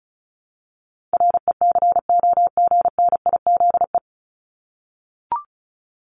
Morse code, heard as a keyed tone, about 34 wpm, 700 Hz. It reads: RECOGNIZE
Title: Top 1251-1500 Words 35wpm